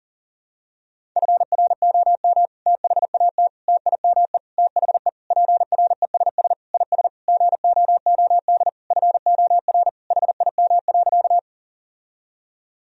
Morse code, heard as a keyed tone, about 40 wpm, 700 Hz. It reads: FROM THAT TIME THE PRESS IS GOOD FOR HIM.
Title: Sentences from Top 1000 Words - Review Part 1 - 40wpm